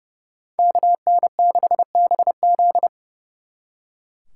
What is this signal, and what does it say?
Morse code, heard as a keyed tone, about 39 wpm, 700 Hz. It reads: KN6BZ